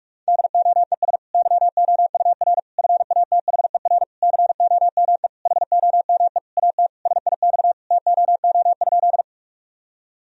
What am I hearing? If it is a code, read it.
Morse code, heard as a keyed tone, about 45 wpm, 700 Hz. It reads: DOES YOUR FATHER COME HOME AT SIX TOO?
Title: Sentences from Top 1000 Words - Review Part 2 - 45wpm